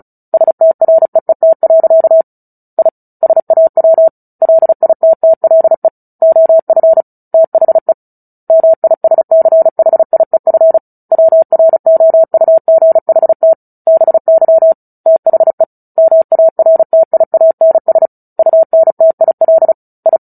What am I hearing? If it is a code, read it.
Morse code, heard as a keyed tone, about 34 wpm, 650 Hz. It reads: STREET. I SAW LITTLE OF THE MISCHIEF WROUGHT BY THE MARTIANS UNTIL I
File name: Book Two / 09-Chapter 9 - 35 WPM